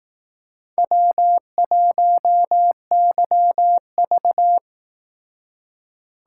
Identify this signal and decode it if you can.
Morse code, heard as a keyed tone, about 18 wpm, 700 Hz. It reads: W1YV
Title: Speed Racing - USA 1x2 Call Signs 15wpm